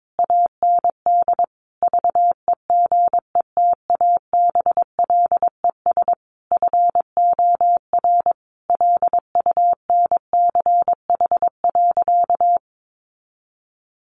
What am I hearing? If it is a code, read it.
Morse code, heard as a keyed tone, about 22 wpm, 700 Hz. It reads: AND VEGETABLES FOR LUNCH.
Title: Easy Phrases 22wpm